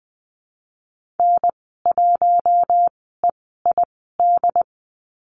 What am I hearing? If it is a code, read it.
Morse code, heard as a keyed tone, about 20 wpm, spaced as for 15 wpm, 700 Hz. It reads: N1EID
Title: 082 Lesson 14g - Number 2 - Call Signs 20wpm